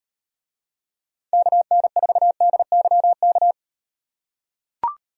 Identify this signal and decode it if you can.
Morse code, heard as a keyed tone, about 38 wpm, 700 Hz. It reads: KN4DYK